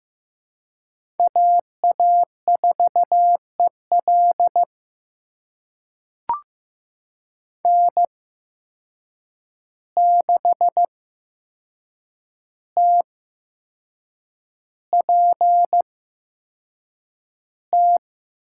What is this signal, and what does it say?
Morse code, heard as a keyed tone, about 15 wpm, spaced as for 6 wpm, 700 Hz. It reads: AA4EL N6TPT
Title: USA Call Signs - 5 Characters - 8x Character Spacing - 15wpm